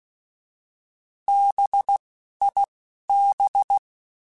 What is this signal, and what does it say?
Morse code, heard as a keyed tone, about 16 wpm, spaced as for 11 wpm, 800 Hz. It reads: BIB